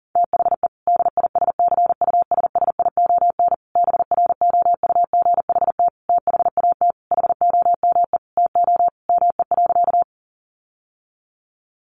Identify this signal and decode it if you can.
Morse code, heard as a keyed tone, about 40 wpm, 700 Hz. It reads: THE DISCUSSION BROUGHT THAT HOME TO ME.
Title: Sentences from Top 1100 Words 40wpm